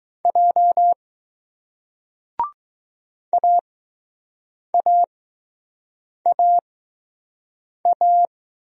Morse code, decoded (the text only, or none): J AAAA